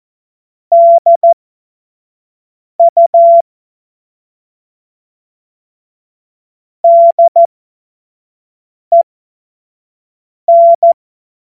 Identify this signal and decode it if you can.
Morse code, heard as a keyed tone, about 14 wpm, spaced as for 5 wpm, 680 Hz. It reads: DU DEN
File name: Morsekurs / Witze / Witz 07. September 2021, 25 F70